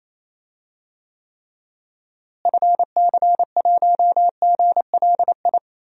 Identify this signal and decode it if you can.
Morse code, heard as a keyed tone, about 28 wpm, 700 Hz. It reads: FC1GLI
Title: DXCC-100 Call Signs - 6 Characters - Repeated 2x - 28wpm